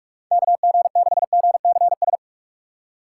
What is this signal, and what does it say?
Morse code, heard as a keyed tone, about 44 wpm, 700 Hz. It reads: KG6GCS
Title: USA Call Signs - 6 Characters - 4x Character Spacing - 45wpm